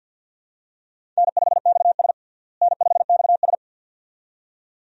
Morse code, heard as a keyed tone, about 47 wpm, 700 Hz. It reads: N5XS N5XS
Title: USA Call Signs - 4 Characters - Repeated 2x - 50wpm